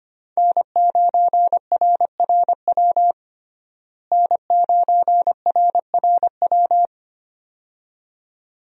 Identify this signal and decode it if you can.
Morse code, heard as a keyed tone, about 25 wpm, 700 Hz. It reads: N9RRW N9RRW
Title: USA Call Signs - 5 Characters - Repeated 2x - 25wpm